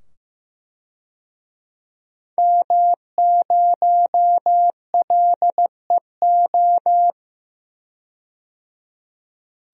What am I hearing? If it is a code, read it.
Morse code, heard as a keyed tone, about 15 wpm, 700 Hz. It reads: M0LEO
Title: DXCC-100 Call Signs - 5 Characters - 15wpm